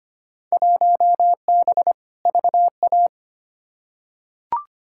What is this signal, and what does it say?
Morse code, heard as a keyed tone, about 25 wpm, 700 Hz. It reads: 1B VA
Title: Field Day Exchanges - 25wpm